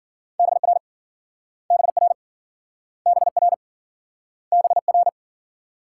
Morse code, heard as a keyed tone, about 53 wpm, 700 Hz. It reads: BR BR BR BR